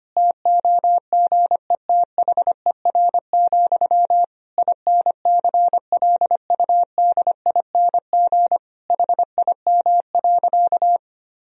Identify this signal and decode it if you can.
Morse code, heard as a keyed tone, about 25 wpm, 700 Hz. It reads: TOGETHER, INCLUDING HIM.
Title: Sentences from Top 1000 Words - Review Part 4 - 25wpm